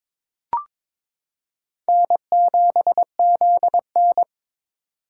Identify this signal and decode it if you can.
Morse code, heard as a keyed tone, about 21 wpm, 700 Hz. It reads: N7ZN
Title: K1USN SST QSO Practice - 22wpm